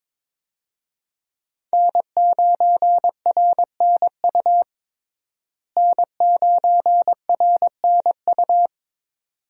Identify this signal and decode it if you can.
Morse code, heard as a keyed tone, about 22 wpm, 700 Hz. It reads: N9RNU N9RNU
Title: USA Call Signs - 5 Characters - Repeated 2x - 22wpm